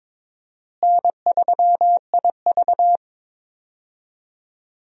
Morse code, heard as a keyed tone, about 22 wpm, 700 Hz. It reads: N3IV
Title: USA Call Signs - 4 Characters - Repeated 2x - 22wpm